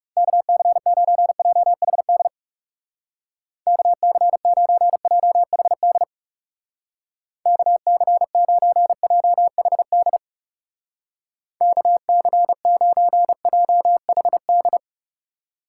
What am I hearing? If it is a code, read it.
Morse code, heard as a keyed tone, about 37 wpm, 700 Hz. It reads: KC9JHD KC9JHD KC9JHD KC9JHD
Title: Speed Racing - USA 2x3 Call Signs 30wpm